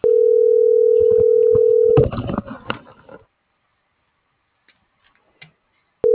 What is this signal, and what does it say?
Ringback tone (North America), 440 + 480 Hz.